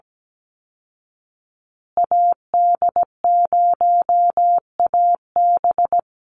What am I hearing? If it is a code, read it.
Morse code, heard as a keyed tone, about 17 wpm, 700 Hz. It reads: AD0AB
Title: Call Signs Repeated 3x - 17wpm